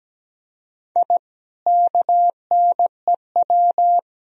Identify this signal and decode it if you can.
Morse code, heard as a keyed tone, about 17 wpm, 700 Hz. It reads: I KNEW
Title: Sentences from Top 1000 Words - Review Part 3 - 17wpm